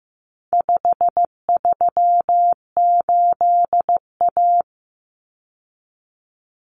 Morse code, heard as a keyed tone, about 15 wpm, 700 Hz. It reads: 538A
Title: Common Cut Numbers - 4-Digit 15wpm